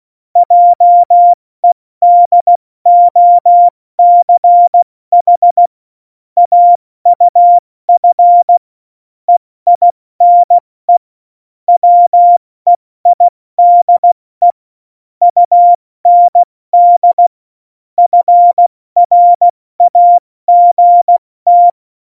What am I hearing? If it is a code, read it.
Morse code, heard as a keyed tone, about 16 wpm, 700 Hz. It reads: JEDOCH AUF EINE WEIDE UND FRAGT
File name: Morsekurs / Witze / Witz 14 Juli 2020, 70 F80